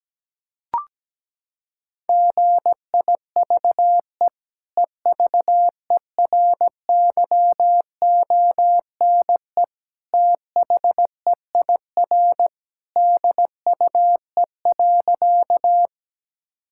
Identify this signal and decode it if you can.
Morse code, heard as a keyed tone, about 17 wpm, 700 Hz. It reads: GIVE EVERYONE THEIR DUE.
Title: Sentences from Top 1000 Words - Review Part 6 - 17wpm